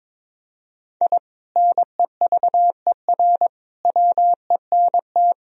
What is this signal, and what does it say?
Morse code, heard as a keyed tone, about 22 wpm, 700 Hz. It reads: I NEVER WENT
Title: Sentences from Top 1000 Words - Review Part 3 - 22wpm